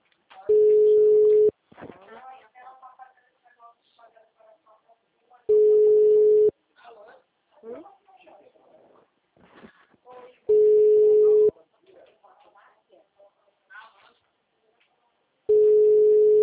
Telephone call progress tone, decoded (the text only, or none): ringback tone